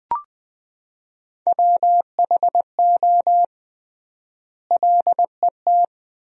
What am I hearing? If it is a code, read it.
Morse code, heard as a keyed tone, about 21 wpm, 700 Hz. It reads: WHO LET
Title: 3x Word-Spacing - Sentences from Top 300 Words - 20wpm